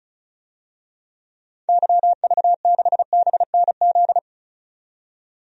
Morse code, heard as a keyed tone, about 34 wpm, 700 Hz. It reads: YV6BNZ
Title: DXCC-100 Call Signs - 6 Characters - 35wpm